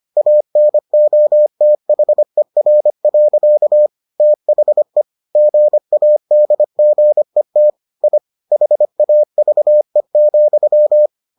Morse code, heard as a keyed tone, about 25 wpm, 600 Hz. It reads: ANOTHER. THE GADGET I HAVE,